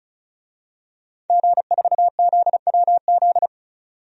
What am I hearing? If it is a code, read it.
Morse code, heard as a keyed tone, about 36 wpm, 700 Hz. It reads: G4ZWZ
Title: DXCC-100 Call Signs - 5 Characters - 35wpm